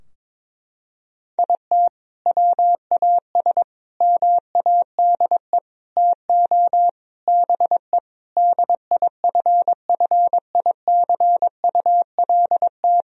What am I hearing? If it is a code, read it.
Morse code, heard as a keyed tone, about 21 wpm, 700 Hz. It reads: IT WAS MADE TO BE DIFFICULT
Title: Sentences from Top 1000 Words - Review Part 4 - 22wpm